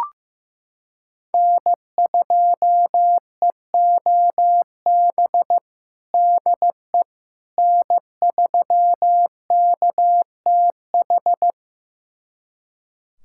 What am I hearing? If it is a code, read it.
Morse code, heard as a keyed tone, about 15 wpm, 700 Hz. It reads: N2EOB DE N3KTH
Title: USA Callsign DE Callsign - 5 Characters - 15wpm